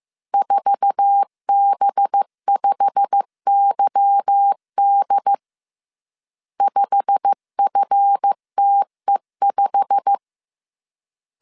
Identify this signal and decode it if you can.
Morse code, heard as a keyed tone, about 15 wpm, 800 Hz. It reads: 4B5YD 5FTE5
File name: CW / DET / 1 9 1500C S C 14wpm 3245